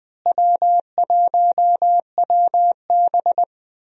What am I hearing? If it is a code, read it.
Morse code, heard as a keyed tone, about 20 wpm, 700 Hz. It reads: W1WB
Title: USA Call Signs - 4 Characters - Repeated 3x - 20wpm